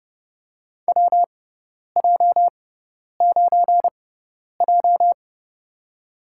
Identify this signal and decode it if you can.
Morse code, heard as a keyed tone, about 30 wpm, spaced as for 10 wpm, 700 Hz. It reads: WJ9J
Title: USA Call Signs - 4 Characters - 6x Character Spacing - 30wpm